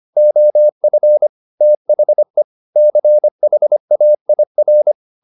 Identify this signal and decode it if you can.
Morse code, heard as a keyed tone, about 25 wpm, 600 Hz. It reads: OF THE CHAIR
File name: the old man and the sea part 1 25wpm